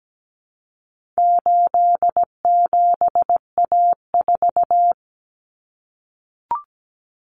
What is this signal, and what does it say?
Morse code, heard as a keyed tone, about 17 wpm, 700 Hz. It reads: 87A4
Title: Common Cut Numbers - 4-Digit 17wpm